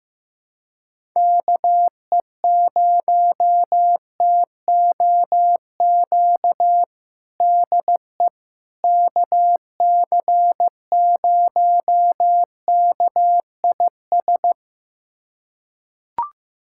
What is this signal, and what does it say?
Morse code, heard as a keyed tone, about 15 wpm, 700 Hz. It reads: KE0TOQ DE KC0KIS
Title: USA Callsign DE Callsign - 6 Characters - 15wpm